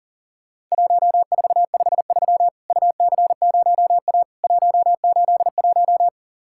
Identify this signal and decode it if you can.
Morse code, heard as a keyed tone, about 40 wpm, 700 Hz. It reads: 1453 UC0A 181
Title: Worked All Europe DX Contest - QTC Exchange - 40wpm